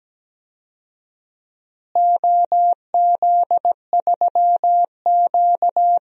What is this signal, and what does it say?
Morse code, heard as a keyed tone, about 17 wpm, 700 Hz. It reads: OZ3Q
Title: DXCC-100 Call Signs - 4 Characters - Repeated 3x - 17wpm